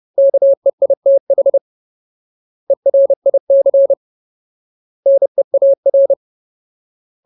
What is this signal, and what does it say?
Morse code, heard as a keyed tone, about 30 wpm, 550 Hz. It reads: KEITH ERIC NEAR
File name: Ultimate Word List / 30WPM Ultimate